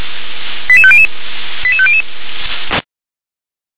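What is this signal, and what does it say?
Five-tone selective call (ZVEI), heard as quick runs of five tones, tones of about 70 ms. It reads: 88500 88500
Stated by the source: Five-tone ZVEI selective call received by SDR, channel 357.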